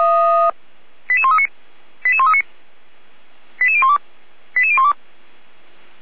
Five-tone selective call (ZVEI), heard as quick runs of five tones, tones of about 70 ms. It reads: 88139 88139 88012 88012